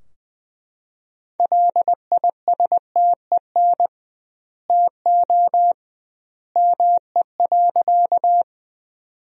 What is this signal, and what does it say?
Morse code, heard as a keyed tone, about 21 wpm, 700 Hz. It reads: LISTEN TO ME.
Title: 2x Word-Spacing - Sentences from Top 400 Words - 20wpm